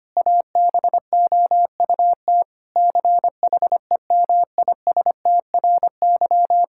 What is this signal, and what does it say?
Morse code, heard as a keyed tone, about 25 wpm, 700 Hz. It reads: ABOUT CHEMISTRY
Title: Sentences from Top 1000 Words - Review Part 6 - 25wpm